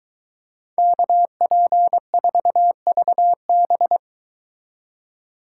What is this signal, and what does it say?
Morse code, heard as a keyed tone, about 23 wpm, 700 Hz. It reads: KP4VB